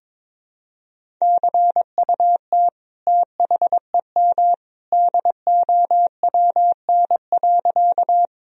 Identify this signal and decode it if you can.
Morse code, heard as a keyed tone, about 22 wpm, 700 Hz. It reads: CUT THEM DOWN.
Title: Sentences from Top 1000 Words - Review Part 1 - 22wpm